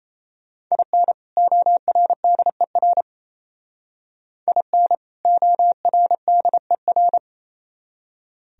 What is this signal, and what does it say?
Morse code, heard as a keyed tone, about 33 wpm, 700 Hz. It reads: IN ORDER IN ORDER
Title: Speed-Racing - Sets of 2 Words 28wpm